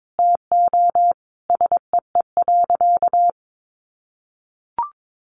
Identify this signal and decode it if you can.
Morse code, heard as a keyed tone, about 22 wpm, 700 Hz. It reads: TO SEE.
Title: Sentences from Top 500 Words 22wpm